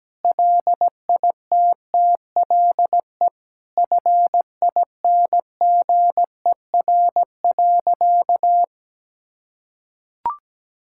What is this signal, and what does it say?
Morse code, heard as a keyed tone, about 17 wpm, 700 Hz. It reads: LITTLE FINGER.
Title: Sentences from Top 1600 Words 17wpm